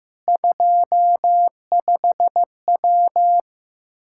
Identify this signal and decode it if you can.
Morse code, heard as a keyed tone, about 16 wpm, 700 Hz. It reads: 25W
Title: Practice QSO - Exchange 2 - Info - 15wpm